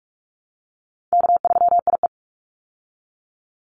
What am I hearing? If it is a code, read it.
Morse code, heard as a keyed tone, about 44 wpm, 700 Hz. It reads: K3IE
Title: Call Signs Repeated 3x - 45wpm